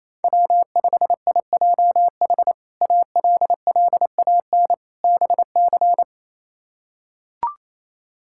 Morse code, heard as a keyed tone, about 27 wpm, 700 Hz. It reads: W5IJH ALLAN BC
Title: K1USN SST QSO Practice - 28wpm